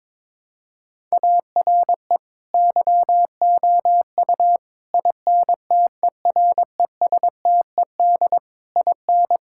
Morse code, heard as a keyed tone, about 22 wpm, 700 Hz. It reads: ARE YOU INTERESTED IN
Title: Sentences from Top 1000 Words - Review Part 6 - 22wpm